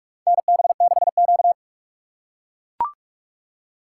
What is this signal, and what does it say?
Morse code, heard as a keyed tone, about 41 wpm, 700 Hz. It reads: NB6Q
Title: USA Call Signs - 4 Characters - 3x Character Spacing - 45wpm